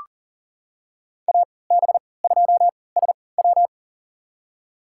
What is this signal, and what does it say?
Morse code, heard as a keyed tone, about 40 wpm, spaced as for 23 wpm, 700 Hz. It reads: AB2SW
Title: USA Call Signs - 5 Characters - 3x Character Spacing - 40wpm